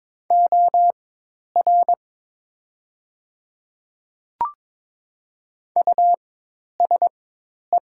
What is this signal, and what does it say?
Morse code, heard as a keyed tone, about 22 wpm, spaced as for 10 wpm, 700 Hz. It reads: OR USE
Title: 4x Character Spacing - Top 200 Words - 22wpm